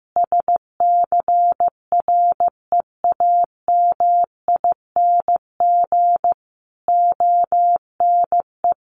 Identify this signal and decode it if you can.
Morse code, heard as a keyed tone, about 15 wpm, 700 Hz. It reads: SCREAMING ONE
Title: A Sound Of Thunder 15wpm